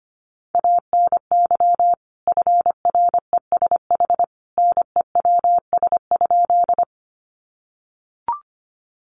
Morse code, heard as a keyed tone, about 25 wpm, 700 Hz. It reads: ANY FRESH NEWS?
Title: Sentences from Top 1200 Words 25wpm